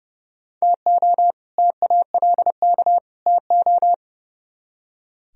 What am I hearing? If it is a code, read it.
Morse code, heard as a keyed tone, about 39 wpm, 700 Hz. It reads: TO TALK TO